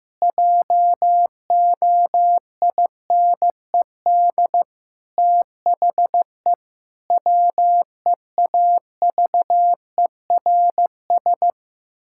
Morse code, heard as a keyed tone, about 15 wpm, 700 Hz. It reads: JOINED THE WEAVERS
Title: Pangram Challenge v2 - 15wpm